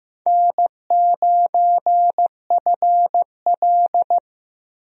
Morse code, heard as a keyed tone, about 15 wpm, 700 Hz. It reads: N9FL
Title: USA Call Signs - 4 Characters - Repeated 3x - 15wpm